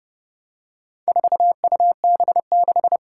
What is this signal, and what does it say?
Morse code, heard as a keyed tone, about 30 wpm, 700 Hz. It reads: 4UB6
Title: Four Challenging Letter-Number Combinations 30wpm